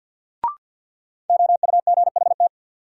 Morse code, heard as a keyed tone, about 53 wpm, 700 Hz. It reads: OUGHT
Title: Sets of 5 Words Encore - New Word Drill - 50wpm